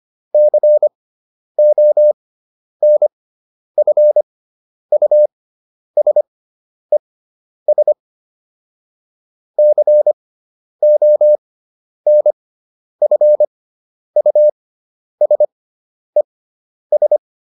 Morse code, heard as a keyed tone, about 25 wpm, spaced as for 10 wpm, 600 Hz. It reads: CONFUSES CONFUSES